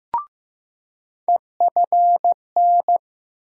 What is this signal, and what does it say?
Morse code, heard as a keyed tone, about 15 wpm, 700 Hz. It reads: EFN
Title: USA Call Sign Postfixes - 3 Letters 15wpm